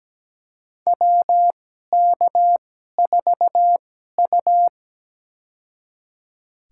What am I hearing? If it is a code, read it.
Morse code, heard as a keyed tone, about 17 wpm, spaced as for 12 wpm, 700 Hz. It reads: WK4U
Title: USA Call Signs - 2x Character Spacing - 17wpm